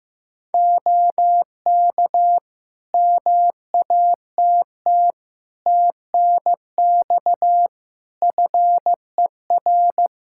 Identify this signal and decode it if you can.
Morse code, heard as a keyed tone, about 15 wpm, 700 Hz. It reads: OK MATT TNX FER
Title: Practice QSO - Exchange 4 - Ending - 15wpm